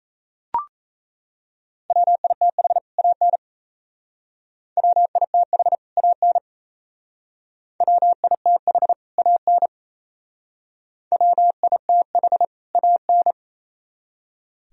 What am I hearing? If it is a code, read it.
Morse code, heard as a keyed tone, about 38 wpm, 690 Hz. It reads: WITH AN WITH AN WITH AN WITH AN